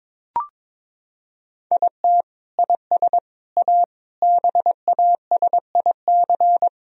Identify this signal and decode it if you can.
Morse code, heard as a keyed tone, about 22 wpm, 700 Hz. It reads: IT IS A BASIC
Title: Sentences from Top 1000 Words - Review Part 3 - 22wpm